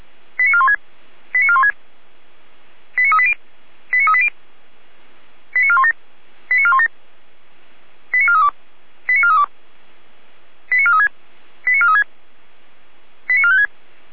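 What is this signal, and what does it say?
Five-tone selective call (ZVEI), heard as quick runs of five tones, tones of about 70 ms. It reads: 89416 89416 89380 89380 89417 89417 89432 89432 89436 89436 89456
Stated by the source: Five-tone ZVEI selective call received on SDR.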